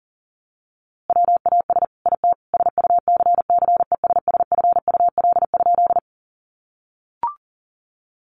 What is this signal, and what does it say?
Morse code, heard as a keyed tone, about 37 wpm, 700 Hz. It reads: WAS IT SUCCESSFUL?
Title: A Sound Of Thunder 40wpm